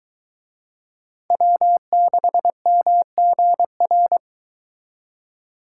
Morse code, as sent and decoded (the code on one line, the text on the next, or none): .-- -.... -- --. .-.
W6MGR